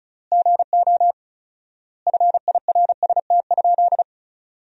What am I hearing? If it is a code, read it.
Morse code, heard as a keyed tone, about 36 wpm, 700 Hz. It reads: GO FIRST?
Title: 4x Word-Spacing - Sentences from Top 100 Words - 35wpm